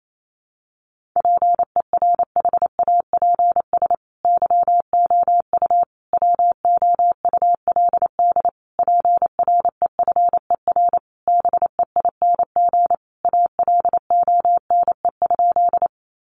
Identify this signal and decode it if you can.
Morse code, heard as a keyed tone, about 28 wpm, 700 Hz. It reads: PERHAPS YOU WOULD PREFER BEING ALONE?
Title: Sentences from Top 1100 Words 28wpm